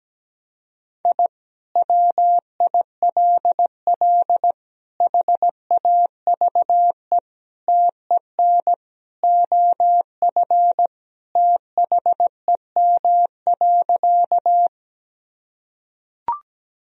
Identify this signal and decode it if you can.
Morse code, heard as a keyed tone, about 17 wpm, 700 Hz. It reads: I WILL HAVE TEN OF THEM.
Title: Sentences from Top 1000 Words - Review Part 2 - 17wpm